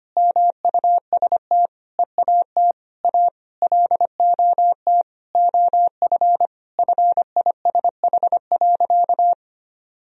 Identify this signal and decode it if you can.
Morse code, heard as a keyed tone, about 25 wpm, 700 Hz. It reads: MUST EAT A LOT OF FISH.
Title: Sentences from Top 1000 Words - Review Part 3 - 25wpm